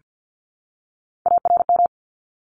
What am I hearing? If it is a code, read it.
Morse code, heard as a keyed tone, about 50 wpm, 700 Hz. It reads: ARM
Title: ES:S Suffix - Top 100 Words with Base Words 50wpm